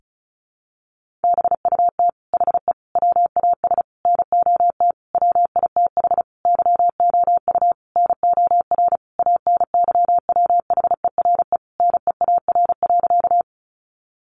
Morse code, heard as a keyed tone, about 34 wpm, 700 Hz. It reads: BUT HE WAS NOT WITH YOU NOR ANYWHERE NEAR.
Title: Sentences from Top 1500 Words 35wpm